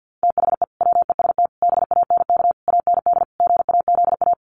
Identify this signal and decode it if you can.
Morse code, heard as a keyed tone, about 50 wpm, 700 Hz. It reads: THE WEST BANK AND GAZA
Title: Sets of 5 Words 50wpm